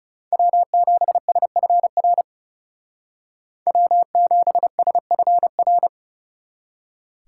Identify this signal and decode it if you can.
Morse code, heard as a keyed tone, about 39 wpm, 700 Hz. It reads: W7SFR W7SFR